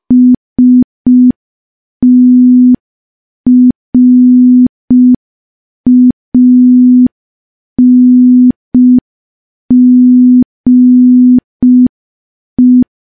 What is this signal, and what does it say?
Morse code, heard as a keyed tone, about 5 wpm, 250 Hz. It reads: STRANGE
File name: morse / morse-training-05WPM